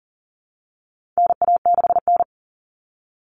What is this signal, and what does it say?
Morse code, heard as a keyed tone, about 40 wpm, 700 Hz. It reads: NA6N
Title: Common Cut Numbers - 4-Digit 40wpm